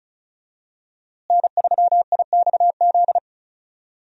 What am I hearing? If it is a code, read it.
Morse code, heard as a keyed tone, about 36 wpm, 700 Hz. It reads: N3IXZ
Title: USA Call Signs - 5 Characters - Repeated 2x - 35wpm